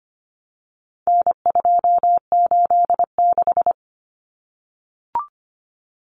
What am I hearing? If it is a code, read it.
Morse code, heard as a keyed tone, about 25 wpm, 700 Hz. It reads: N286
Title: Common Cut Numbers - 4-Digit 25wpm